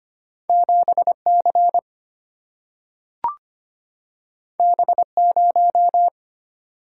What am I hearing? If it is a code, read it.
Morse code, heard as a keyed tone, about 25 wpm, 700 Hz. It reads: 7C B0
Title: Single-Letter Single-Number 25wpm